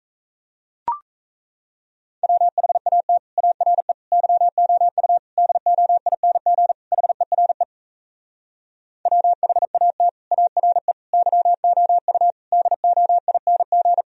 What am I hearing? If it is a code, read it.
Morse code, heard as a keyed tone, about 37 wpm, 700 Hz. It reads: WHAT ARE YOU DOING HERE WHAT ARE YOU DOING